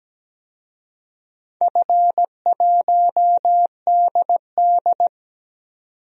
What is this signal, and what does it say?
Morse code, heard as a keyed tone, about 17 wpm, 700 Hz. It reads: F1DD
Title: DXCC-100 Call Signs - 4 Characters - 17wpm